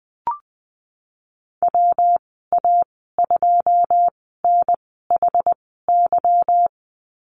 Morse code, heard as a keyed tone, about 20 wpm, spaced as for 15 wpm, 700 Hz. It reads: WA2NHY
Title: 200 Lesson 32g - Letter k - Call Signs 20wpm